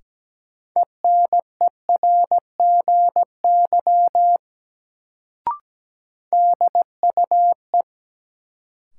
Morse code, heard as a keyed tone, about 17 wpm, 700 Hz. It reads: ENERGY DUE
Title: Sentences from Top 900 Words - New Word Drill - 17wpm